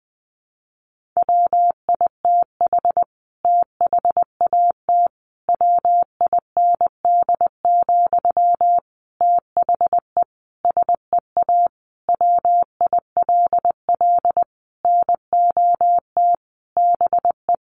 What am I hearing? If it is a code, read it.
Morse code, heard as a keyed tone, about 19 wpm, 700 Hz. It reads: WITH THAT WIND, THE SEA WILL NOT BE
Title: Sentences from Top 1400 Words 20wpm